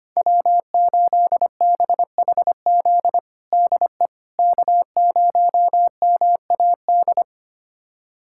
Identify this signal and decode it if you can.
Morse code, heard as a keyed tone, about 25 wpm, 700 Hz. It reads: W8BHZ DE K0MAD
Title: USA Callsign DE Callsign - 5 Characters - 25wpm